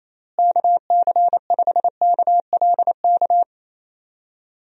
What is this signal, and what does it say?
Morse code, heard as a keyed tone, about 28 wpm, 700 Hz. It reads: KC5KLK
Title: USA Call Signs - 6 Characters - Repeated 2x - 28wpm